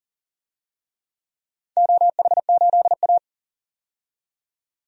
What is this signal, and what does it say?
Morse code, heard as a keyed tone, about 40 wpm, 700 Hz. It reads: OH8A
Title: DXCC-100 Call Signs - 4 Characters - 40wpm